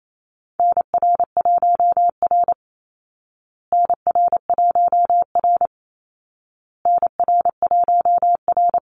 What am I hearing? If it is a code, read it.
Morse code, heard as a keyed tone, about 28 wpm, 700 Hz. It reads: NR1R NR1R NR1R
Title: Call Signs Repeated 2x - 28wpm